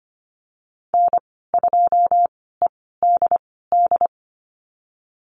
Morse code, heard as a keyed tone, about 25 wpm, spaced as for 16 wpm, 690 Hz. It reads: N2EDD
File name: Lesson 15g - Number 5 - Call Signs 25wpm